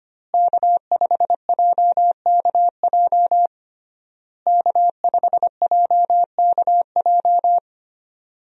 Morse code, heard as a keyed tone, about 25 wpm, 700 Hz. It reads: K5JKJ K5JKJ
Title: USA Call Signs - 5 Characters - Repeated 3x - 25wpm